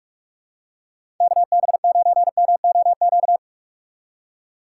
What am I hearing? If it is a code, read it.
Morse code, heard as a keyed tone, about 46 wpm, 700 Hz. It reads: KB9MOQ
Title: USA Call Signs - 6 Characters - 6x Character Spacing - 45wpm